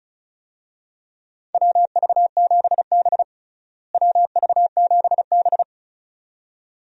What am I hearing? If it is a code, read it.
Morse code, heard as a keyed tone, about 36 wpm, 700 Hz. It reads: WV7B WV7B
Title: USA Call Signs - 4 Characters - Repeated 2x - 35wpm